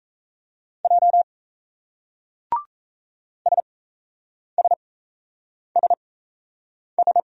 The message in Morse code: J SSSS

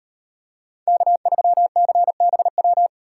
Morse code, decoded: K3CBW

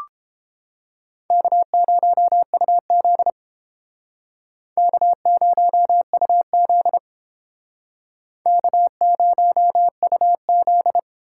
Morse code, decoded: K0UZ K0UZ K0UZ

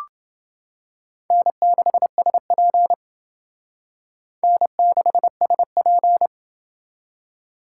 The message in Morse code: N6SP N6SP

